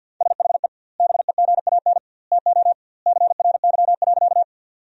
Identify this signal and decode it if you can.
Morse code, heard as a keyed tone, about 50 wpm, 700 Hz. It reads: SHE BEGAN TO CRY.
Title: 2x Word-Spacing - Sentences from Top 500 Words - 50wpm